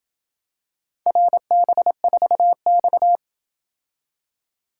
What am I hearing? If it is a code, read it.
Morse code, heard as a keyed tone, about 26 wpm, 700 Hz. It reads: RB4X